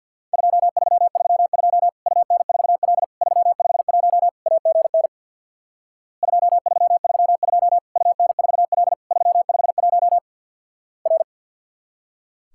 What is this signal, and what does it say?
Morse code, heard as a keyed tone, about 50 wpm, 700 Hz. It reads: 2332 UN4L 351 AGN 2332 UN4L 351 R
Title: Worked All Europe DX Contest - QTC Exchange with Repeats - 50wpm